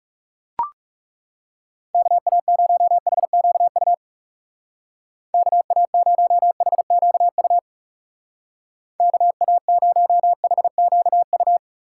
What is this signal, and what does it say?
Morse code, heard as a keyed tone, about 37 wpm, 700 Hz. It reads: KA0HQU KA0HQU KA0HQU
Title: Speed Racing - USA 2x3 Call Signs 30wpm